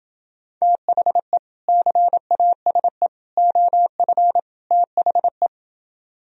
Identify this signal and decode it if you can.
Morse code, heard as a keyed tone, about 28 wpm, 700 Hz. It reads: THE CASE OF THE